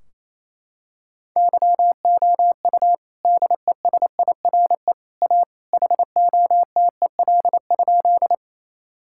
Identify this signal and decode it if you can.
Morse code, heard as a keyed tone, about 28 wpm, 700 Hz. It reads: YOU DESIRE A HOTEL?
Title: Sentences from Top 1600 Words 28wpm